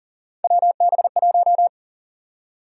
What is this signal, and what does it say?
Morse code, heard as a keyed tone, about 40 wpm, 700 Hz. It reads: WB1
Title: USA Call Sign Prefixes - 40wpm